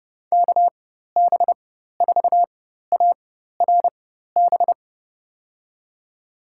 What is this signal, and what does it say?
Morse code, heard as a keyed tone, about 30 wpm, spaced as for 14 wpm, 700 Hz. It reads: KB4ARB
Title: USA Call Signs - 4x Character Spacing - 30wpm